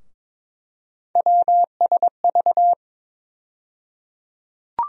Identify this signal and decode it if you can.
Morse code, heard as a keyed tone, about 22 wpm, 700 Hz. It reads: WSV
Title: USA Call Sign Postfixes - 3 Letters 22wpm